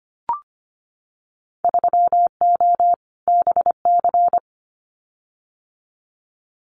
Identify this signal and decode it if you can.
Morse code, heard as a keyed tone, about 25 wpm, 700 Hz. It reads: 3O BC
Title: Winter Field Day Exchanges Encore 25wpm